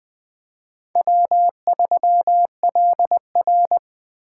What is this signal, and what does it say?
Morse code, heard as a keyed tone, about 20 wpm, 700 Hz. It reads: W3LR
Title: Speed Racing - USA 1x2 Call Signs 17wpm